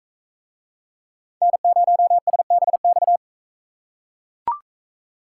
Morse code, heard as a keyed tone, about 42 wpm, 700 Hz. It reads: N0SBX